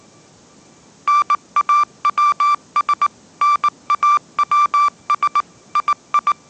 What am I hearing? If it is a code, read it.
Morse code, heard as a keyed tone, about 19 wpm, 1170 Hz. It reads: NAWS NAWS II
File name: 20wpm